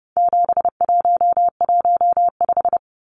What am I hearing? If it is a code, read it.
Morse code, heard as a keyed tone, about 30 wpm, 700 Hz. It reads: 7115
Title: Numbers - 4-Digit 30wpm